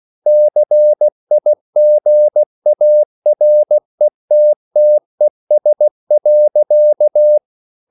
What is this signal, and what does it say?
Morse code, heard as a keyed tone, about 16 wpm, 600 Hz. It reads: CIGARETTES.